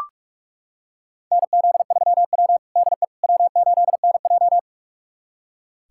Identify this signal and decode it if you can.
Morse code, heard as a keyed tone, about 41 wpm, 700 Hz. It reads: NZ3W DE W8NJ
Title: USA Callsign DE Callsign - 4 Characters - 45wpm